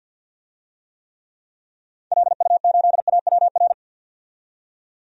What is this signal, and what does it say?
Morse code, heard as a keyed tone, about 51 wpm, 700 Hz. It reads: LU8AWR
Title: DXCC-100 Call Signs - 6 Characters - 50wpm